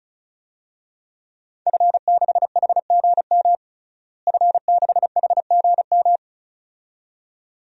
Morse code, heard as a keyed tone, about 36 wpm, 700 Hz. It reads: F6HGM F6HGM
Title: DXCC-100 Call Signs - 5 Characters - Repeated 2x - 35wpm